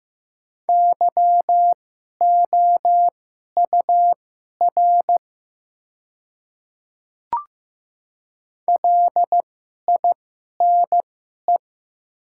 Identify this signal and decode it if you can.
Morse code, heard as a keyed tone, about 15 wpm, spaced as for 11 wpm, 700 Hz. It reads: YOUR LINE
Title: 2x Character Spacing - Top 200 Words - 15wpm